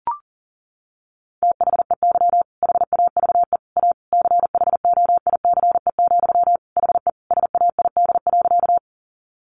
Morse code, heard as a keyed tone, about 37 wpm, 700 Hz. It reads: THEY HAVE A CHOICE, HE SAID.
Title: Sentences from Top 900 Words 40wpm